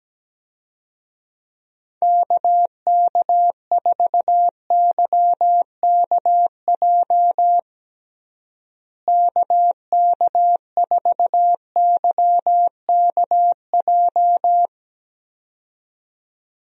Morse code, heard as a keyed tone, about 17 wpm, 700 Hz. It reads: KK4YKJ KK4YKJ
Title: USA Call Signs - 6 Characters - Repeated 2x - 17wpm